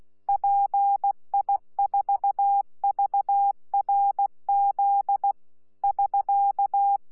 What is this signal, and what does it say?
Morse code, heard as a keyed tone, about 16 wpm, 800 Hz. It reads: PI4VRZ <SK>
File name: Z PI4VRZ / PI4VRZ 14-jun-2025